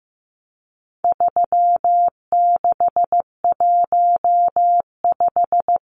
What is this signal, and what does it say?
Morse code, heard as a keyed tone, about 15 wpm, 700 Hz. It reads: 3615
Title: Numbers - 4-Digit 15wpm